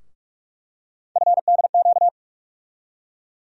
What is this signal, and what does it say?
Morse code, heard as a keyed tone, about 43 wpm, 700 Hz. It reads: FDQ
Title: USA Call Sign Postfixes - 3 Letters 45wpm